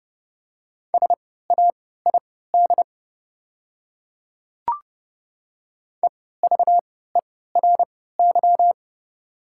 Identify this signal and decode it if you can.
Morse code, heard as a keyed tone, about 30 wpm, spaced as for 17 wpm, 700 Hz. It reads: SAID EVERY
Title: 3x Character Spacing - Top 200 Words - 30wpm